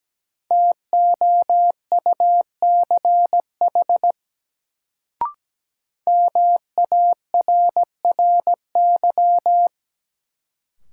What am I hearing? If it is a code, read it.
Morse code, heard as a keyed tone, about 17 wpm, 700 Hz. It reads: TOUCH MARRY
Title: Sentences from Top 1200 Words - New Word Drill Alt - 17wpm